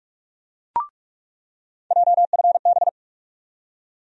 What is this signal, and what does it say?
Morse code, heard as a keyed tone, about 46 wpm, 700 Hz. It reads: JFB
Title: USA Call Sign Postfixes - 3 Letters 45wpm